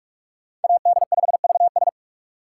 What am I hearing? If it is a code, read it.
Morse code, heard as a keyed tone, about 42 wpm, 700 Hz. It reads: AD5VS
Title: USA Call Signs - 5 Characters - 6x Character Spacing - 45wpm